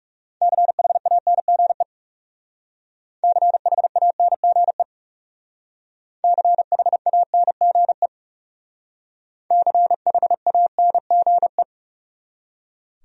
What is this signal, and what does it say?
Morse code, heard as a keyed tone, about 38 wpm, 700 Hz. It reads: CHANGE CHANGE CHANGE CHANGE